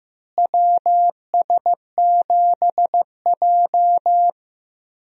Morse code, heard as a keyed tone, about 15 wpm, 700 Hz. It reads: WS7J
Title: USA Call Signs - 4 Characters - Repeated 3x - 15wpm